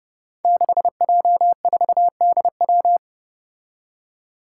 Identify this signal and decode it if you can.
Morse code, heard as a keyed tone, about 30 wpm, 700 Hz. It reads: 6J4DW
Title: Five Challenging Letter-Number Combinations 30wpm